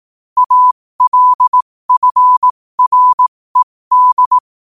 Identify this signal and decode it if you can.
Morse code, heard as a keyed tone, about 18 wpm, spaced as for 15 wpm, 1000 Hz. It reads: ALFRED
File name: qotdfiles / 15wpm